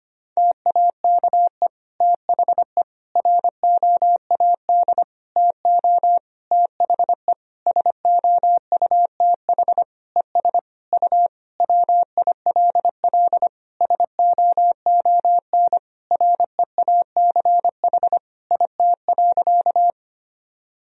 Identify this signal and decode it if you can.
Morse code, heard as a keyed tone, about 25 wpm, 700 Hz. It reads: TAKE THE ROAD TO THE SOUTH ES U WILL SOON REACH IT.
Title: Sentences from Top 400 Words and Abbreviations 25wpm